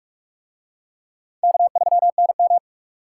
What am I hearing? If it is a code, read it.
Morse code, heard as a keyed tone, about 41 wpm, 700 Hz. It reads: K3NM